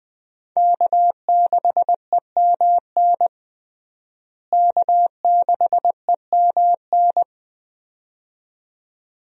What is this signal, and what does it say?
Morse code, heard as a keyed tone, about 21 wpm, 700 Hz. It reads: K6EMN K6EMN
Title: USA Call Signs - 5 Characters - Repeated 2x - 20wpm